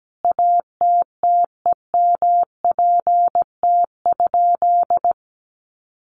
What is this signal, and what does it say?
Morse code, heard as a keyed tone, about 17 wpm, 700 Hz. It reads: ATTEMPT?
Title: Sentences from Top 1000 Words 17wpm